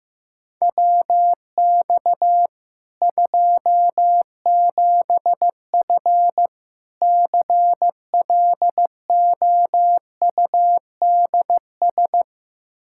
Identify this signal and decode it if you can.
Morse code, heard as a keyed tone, about 15 wpm, 700 Hz. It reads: WX 27F CLOUDS
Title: K1USN SST QSO Practice 15wpm